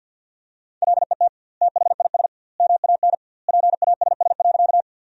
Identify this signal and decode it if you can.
Morse code, heard as a keyed tone, about 50 wpm, 700 Hz. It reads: LET THIS MAN PASS.
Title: 2x Word-Spacing - Sentences from Top 400 Words - 50wpm